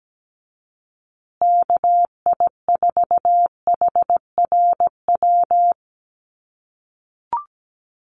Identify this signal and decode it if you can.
Morse code, heard as a keyed tone, about 17 wpm, 700 Hz. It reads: KI4HRW
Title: USA Call Signs Repeated 3x 17wpm